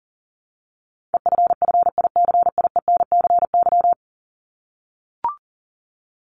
Morse code, heard as a keyed tone, about 40 wpm, 700 Hz. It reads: EFFICIENCY
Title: Top 1501-1750 Words 40wpm